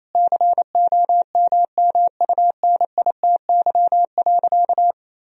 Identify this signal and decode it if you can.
Morse code, heard as a keyed tone, about 28 wpm, 700 Hz. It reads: COMMUNITY.
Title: Sentences from Top 1000 Words - Review Part 4 - 28wpm